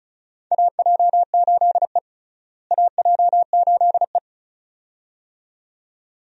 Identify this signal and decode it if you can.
Morse code, heard as a keyed tone, about 34 wpm, 700 Hz. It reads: AJ8E AJ8E
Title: USA Call Signs - 4 Characters - Repeated 2x - 35wpm